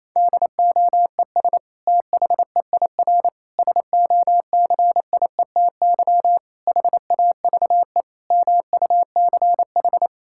Morse code, heard as a keyed tone, about 28 wpm, 700 Hz. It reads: DOES THEIR SOCIETY HAVE MUCH
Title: Sentences from Top 1000 Words - Review Part 6 - 28wpm